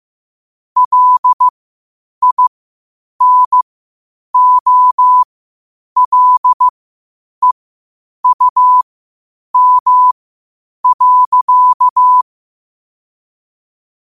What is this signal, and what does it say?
Morse code, heard as a keyed tone, about 15 wpm, spaced as for 8 wpm, 1000 Hz. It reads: LINOLEUM.
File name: qotdfiles / 8wpm